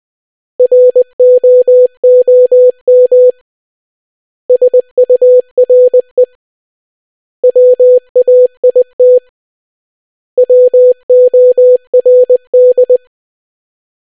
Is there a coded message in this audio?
Yes: Morse code from 0.6 to 13.0 s, heard as a keyed tone, about 20 wpm, 500 Hz. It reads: ROOM SURE WAIT WORD